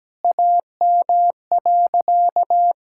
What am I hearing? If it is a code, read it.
Morse code, heard as a keyed tone, about 17 wpm, 700 Hz. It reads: AM.
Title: Sentences from Top 1000 Words - Review Part 6 - 17wpm